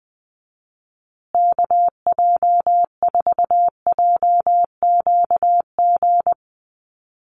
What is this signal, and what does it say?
Morse code, heard as a keyed tone, about 20 wpm, 700 Hz. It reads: KJ4JQG
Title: USA Call Signs Repeated 3x 20wpm